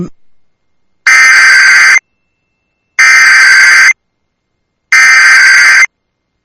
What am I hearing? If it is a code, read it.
Emergency Alert System (SAME) tones (header bursts), heard as harsh buzzing bursts. It is heard from 1.1 to 5.8 s.